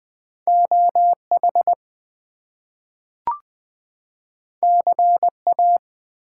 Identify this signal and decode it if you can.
Morse code, heard as a keyed tone, about 20 wpm, 700 Hz. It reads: OH CA
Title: US States and Canadian Provinces 20wpm